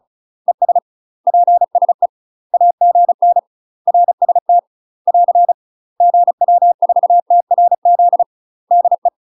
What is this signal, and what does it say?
Morse code, heard as a keyed tone, about 36 wpm, 700 Hz. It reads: ES PSE AGN RST <AR> GW4TRZ DE
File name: UK MASTER QSOs TRAINING / 35WPM / UK MASTER QSOs 0007